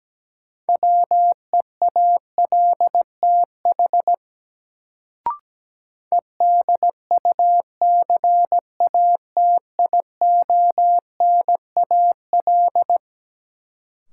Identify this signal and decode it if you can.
Morse code, heard as a keyed tone, about 17 wpm, 700 Hz. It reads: WEALTH EDUCATIONAL
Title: Sentences from Top 1200 Words - New Word Drill Alt - 17wpm